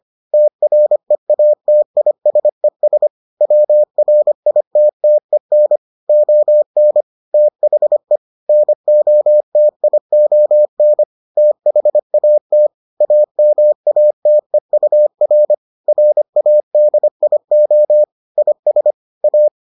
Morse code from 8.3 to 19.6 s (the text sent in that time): NOTION THAT AMATEUR RADIO IS A